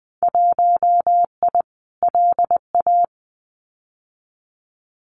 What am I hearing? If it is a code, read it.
Morse code, heard as a keyed tone, about 20 wpm, 700 Hz. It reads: 1I LA
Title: Winter Field Day Exchanges Encore 20wpm